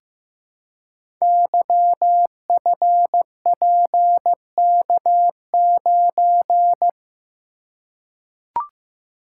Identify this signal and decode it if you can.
Morse code, heard as a keyed tone, about 15 wpm, 700 Hz. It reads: YFPK9
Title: Code Groups - 5 Letter:Numbers - 15wpm